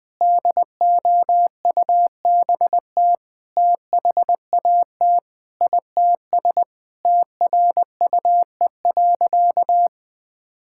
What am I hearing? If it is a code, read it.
Morse code, heard as a keyed tone, about 20 wpm, 700 Hz. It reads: DOUBT THAT ITS TRUE.
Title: Sentences with 7-Word Phrases - 20wpm